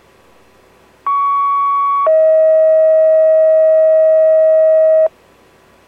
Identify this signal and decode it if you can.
Two-tone page, heard as a one-second tone, then a three-second tone, about 1120 then 640 Hz. It is heard from 1.1 to 5.1 s.